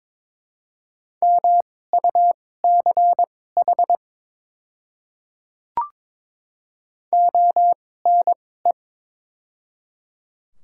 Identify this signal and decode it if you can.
Morse code, heard as a keyed tone, about 21 wpm, spaced as for 16 wpm, 700 Hz. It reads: MUCH ONE
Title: 2x Character Spacing - Top 200 Words - 22wpm